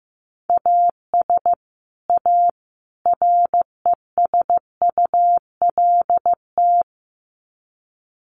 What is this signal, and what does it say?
Morse code, heard as a keyed tone, about 15 wpm, 700 Hz. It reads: AS A RESULT
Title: Sets of 3 Words 15wpm